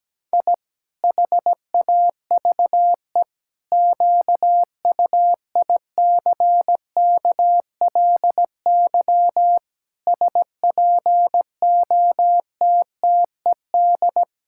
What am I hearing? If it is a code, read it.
Morse code, heard as a keyed tone, about 17 wpm, 700 Hz. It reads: I HAVE QUICKLY SPOTTED
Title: Pangram Challenge v2 - 17wpm